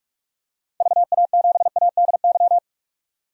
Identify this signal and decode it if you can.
Morse code, heard as a keyed tone, about 46 wpm, 700 Hz. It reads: VA7ADY
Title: DXCC-100 Call Signs - 6 Characters - 45wpm